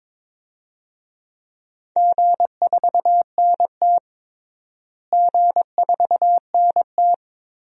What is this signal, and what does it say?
Morse code, heard as a keyed tone, about 22 wpm, 700 Hz. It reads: G4NT G4NT
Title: DXCC-100 Call Signs - 4 Characters - Repeated 2x - 22wpm